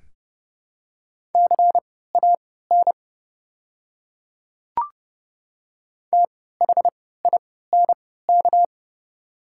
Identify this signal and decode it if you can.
Morse code, heard as a keyed tone, about 30 wpm, spaced as for 17 wpm, 700 Hz. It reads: CAN THINK
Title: 3x Character Spacing - Top 200 Words - 30wpm